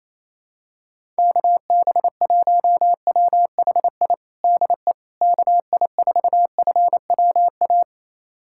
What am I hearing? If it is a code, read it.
Morse code, heard as a keyed tone, about 28 wpm, 700 Hz. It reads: KB1WHI DE KI4FWA
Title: USA Callsign DE Callsign - 6 Characters - 28wpm